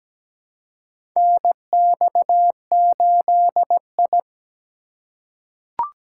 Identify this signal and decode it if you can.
Morse code, heard as a keyed tone, about 17 wpm, 700 Hz. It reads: NX8I
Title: USA Call Signs - 4 Characters - 2x Character Spacing - 17wpm